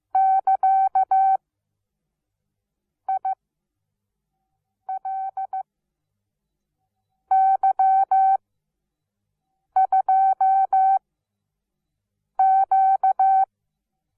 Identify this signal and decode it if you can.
Morse code, heard as a keyed tone, about 15 wpm, spaced as for 5 wpm, 770 Hz. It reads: <KA>ILY2Q
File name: Onlinelehrgang / morsen / DJ4UF Morsen17